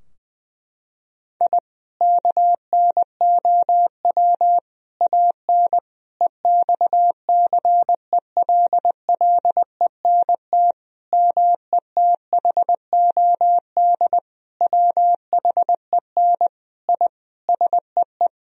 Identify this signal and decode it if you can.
Morse code, heard as a keyed tone, about 20 wpm, 700 Hz. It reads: I KNOW AN EXCELLENT METHOD WHEN I SEE
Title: Sentences from Top 1000 Words - Review Part 3 - 20wpm